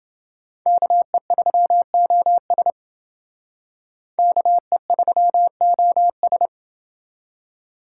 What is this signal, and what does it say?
Morse code, heard as a keyed tone, about 28 wpm, 700 Hz. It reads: KE3OS KE3OS